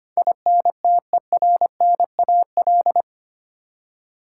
Morse code, heard as a keyed tone, about 25 wpm, 700 Hz. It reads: INTERNAL
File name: Eight Letter Words - 25wpm